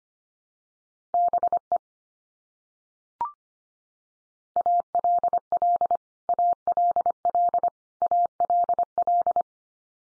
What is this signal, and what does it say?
Morse code, heard as a keyed tone, about 25 wpm, 700 Hz. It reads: BE ALL ALL ALL
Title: Top 100 Words 25wpm x3